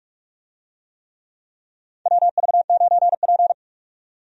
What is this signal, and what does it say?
Morse code, heard as a keyed tone, about 47 wpm, 700 Hz. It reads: WV9P
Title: USA Call Signs - 4 Characters - 45wpm